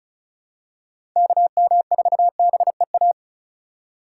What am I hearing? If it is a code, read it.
Morse code, heard as a keyed tone, about 34 wpm, 700 Hz. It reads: KM4BEA
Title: USA Call Signs - 6 Characters - 3x Character Spacing - 35wpm